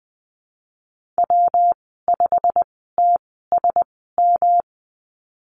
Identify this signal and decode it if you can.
Morse code, heard as a keyed tone, about 20 wpm, spaced as for 15 wpm, 700 Hz. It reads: W5TSM
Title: 147 Lesson 24g - Letter y - Call Signs 20wpm